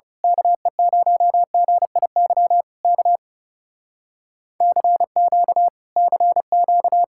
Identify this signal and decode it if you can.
Morse code, heard as a keyed tone, about 28 wpm, 700 Hz. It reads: KE0GIY K CQ CQ
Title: Speed-Racing Common QSO Phrases - 30wpm